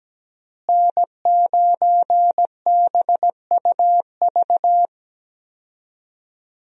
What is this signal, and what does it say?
Morse code, heard as a keyed tone, about 17 wpm, 700 Hz. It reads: N9BUV
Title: USA Call Signs - 5 Characters - Repeated 3x - 17wpm